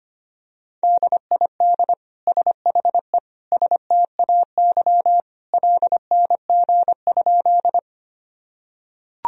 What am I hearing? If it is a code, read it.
Morse code, heard as a keyed tone, about 25 wpm, 700 Hz. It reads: DID SHE STAY LNG?
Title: Sentences from Top 500 Words and Abbreviations 25wpm